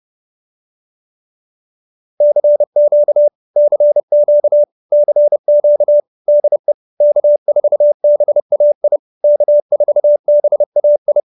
Morse code, heard as a keyed tone, about 30 wpm, 590 Hz. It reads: CQ CQ CQ DE K4BAI K4BAI